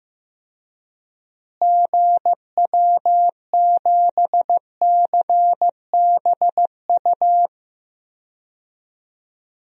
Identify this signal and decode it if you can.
Morse code, heard as a keyed tone, about 15 wpm, 700 Hz. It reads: GW7CBU
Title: DXCC-100 Call Signs - 6 Characters - Repeated 2x - 15wpm